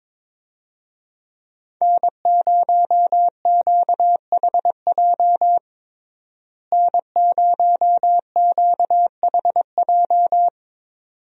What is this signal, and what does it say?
Morse code, heard as a keyed tone, about 22 wpm, 700 Hz. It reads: N0QHJ N0QHJ
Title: USA Call Signs - 5 Characters - Repeated 2x - 22wpm